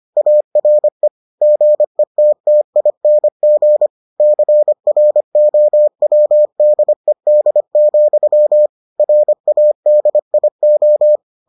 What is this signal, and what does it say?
Morse code, heard as a keyed tone, about 25 wpm, 600 Hz. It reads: ARE GETTING CROWDED, RADIO